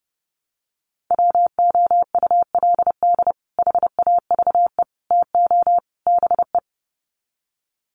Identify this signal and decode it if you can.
Morse code, heard as a keyed tone, about 30 wpm, 700 Hz. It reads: WOULD HAVE TO BE
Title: Sets of 4 Words 30wpm